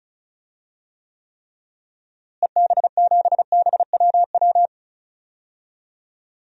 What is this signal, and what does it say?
Morse code, heard as a keyed tone, about 36 wpm, 700 Hz. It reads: EB7BWW
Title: DXCC-100 Call Signs - 6 Characters - Repeated 3x - 35wpm